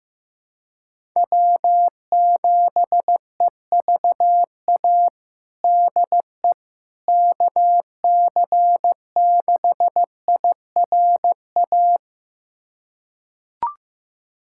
Morse code, heard as a keyed tone, about 15 wpm, 700 Hz. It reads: W7EVA DE KC6IRA
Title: USA Callsign DE Callsign 15wpm